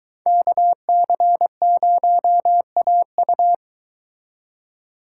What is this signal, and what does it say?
Morse code, heard as a keyed tone, about 22 wpm, 700 Hz. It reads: KC0AU